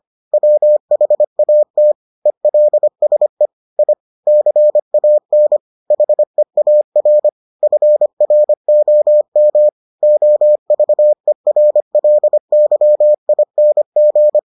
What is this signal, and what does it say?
Morse code, heard as a keyed tone, about 25 wpm, 600 Hz. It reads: WHAT ELSE I CAN HEAR FROM OVERLYING